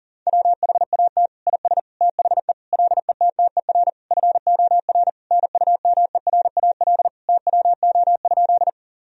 Morse code, heard as a keyed tone, about 37 wpm, 700 Hz. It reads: WHAT IS THE LETTER FOR NUMERAL TWO?
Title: Sentences from Top 1000 Words - Review Part 2 - 40wpm